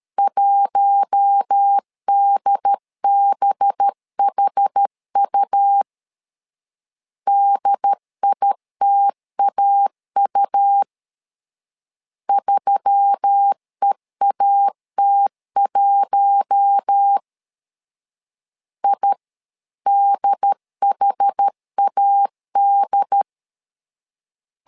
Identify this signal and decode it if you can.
Morse code, heard as a keyed tone, about 13 wpm, 800 Hz. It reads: 1DBHU DITAU 3EAT1 IDHAD
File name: LEZIONI / 5 LBVC / 5 LB 12wpm 1906